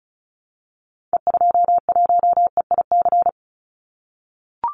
Morse code, heard as a keyed tone, about 34 wpm, 700 Hz. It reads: E21EIC
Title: Call Signs - 35wpm